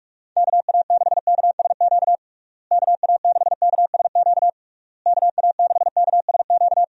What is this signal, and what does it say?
Morse code, heard as a keyed tone, about 46 wpm, 700 Hz. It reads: KA6KSQ KA6KSQ KA6KSQ
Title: USA Call Signs - 6 Characters - Repeated 3x - 45wpm